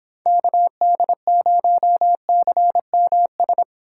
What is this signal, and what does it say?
Morse code, heard as a keyed tone, about 26 wpm, 700 Hz. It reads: KD0CMS
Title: Speed Racing - USA Call Signs 22wpm